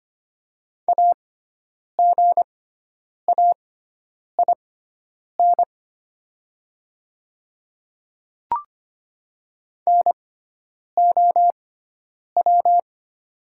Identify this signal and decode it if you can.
Morse code, heard as a keyed tone, about 25 wpm, spaced as for 9 wpm, 700 Hz. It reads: AGAIN NOW
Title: 6x Character Spacing - Top 200 Words - 25wpm